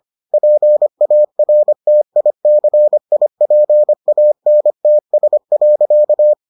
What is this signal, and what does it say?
Morse code, heard as a keyed tone, about 25 wpm, 600 Hz. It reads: PARTICIPANTS.